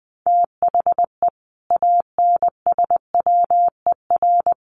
Morse code, heard as a keyed tone, about 20 wpm, 700 Hz. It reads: THE ANSWER
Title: Sentences from Top 1100 Words 20wpm